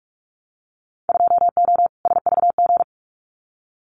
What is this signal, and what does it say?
Morse code, heard as a keyed tone, about 46 wpm, 700 Hz. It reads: 2O SVG
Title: Winter Field Day Exchanges 45wpm